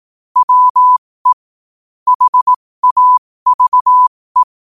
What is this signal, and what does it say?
Morse code, heard as a keyed tone, about 18 wpm, spaced as for 15 wpm, 1000 Hz. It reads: WE HAVE
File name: qotdfiles / 15wpm